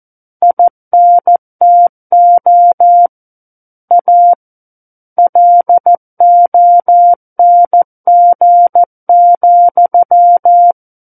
Morse code, heard as a keyed tone, about 14 wpm, 700 Hz. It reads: INTO A LONG,
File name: Book One / 05-Chapter 5 - 14 WPM